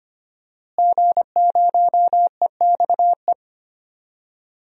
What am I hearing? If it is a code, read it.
Morse code, heard as a keyed tone, about 25 wpm, 700 Hz. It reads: G0EXE
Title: DXCC-100 Call Signs - 5 Characters - Repeated 2x - 25wpm